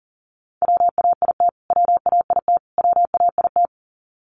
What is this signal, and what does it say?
Morse code, heard as a keyed tone, about 40 wpm, 700 Hz. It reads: WAIT WAIT WAIT
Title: Top 500 Words Repeated 40wpm